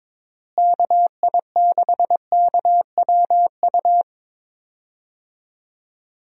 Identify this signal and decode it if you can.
Morse code, heard as a keyed tone, about 22 wpm, 700 Hz. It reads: KI6KWU
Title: USA Call Signs - 6 Characters - Repeated 2x - 22wpm